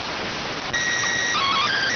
SSTV (slow-scan television), PD 290.